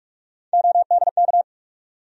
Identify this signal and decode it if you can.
Morse code, heard as a keyed tone, about 43 wpm, 700 Hz. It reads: ODK
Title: USA Call Sign Postfixes - 3 Letters 45wpm